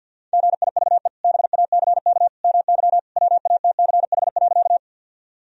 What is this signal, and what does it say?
Morse code, heard as a keyed tone, about 52 wpm, 700 Hz. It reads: GIVE BACK MY WATCH.
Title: Sentences from Top 1000 Words - Review Part 1 - 50wpm